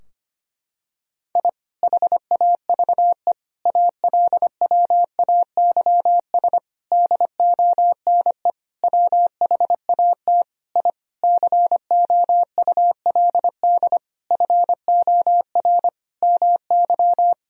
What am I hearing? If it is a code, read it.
Morse code, heard as a keyed tone, about 25 wpm, 700 Hz. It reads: I HAVE ALWAYS DONE WHAT I COULD FOR MY
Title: Sentences from Top 1000 Words - Review Part 2 - 25wpm